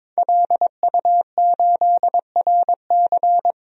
Morse code, heard as a keyed tone, about 22 wpm, 700 Hz. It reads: LU8RC
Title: DXCC-100 Call Signs - 5 Characters - 22wpm